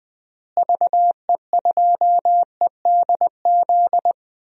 Morse code, heard as a keyed tone, about 20 wpm, 700 Hz. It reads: VE2EDZ
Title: DXCC-100 Call Signs - 6 Characters - Repeated 3x - 20wpm